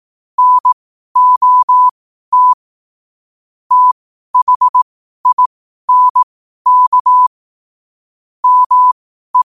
Morse code, heard as a keyed tone, about 18 wpm, spaced as for 12 wpm, 1000 Hz. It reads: NOT THINK ME
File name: qotdfiles / 12wpm